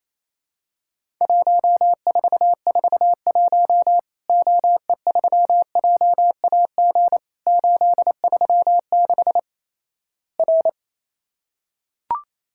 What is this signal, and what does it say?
Morse code, heard as a keyed tone, about 28 wpm, 700 Hz. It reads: 1441 OE3JAG 836 R
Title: Worked All Europe DX Contest - QTC Exchange - 28wpm